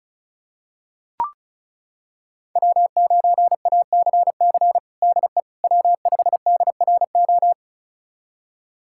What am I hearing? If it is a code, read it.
Morse code, heard as a keyed tone, about 36 wpm, 700 Hz. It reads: W9ACC DE W5DRO
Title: USA Callsign DE Callsign - 5 Characters - 35wpm